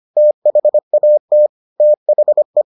Morse code, heard as a keyed tone, about 25 wpm, 600 Hz. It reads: THAT THE